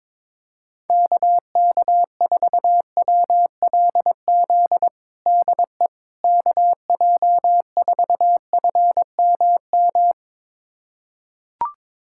Morse code, heard as a keyed tone, about 22 wpm, 700 Hz. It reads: KK4WLZ DE KJ4FMM
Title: USA Callsign DE Callsign - 6 Characters - 22wpm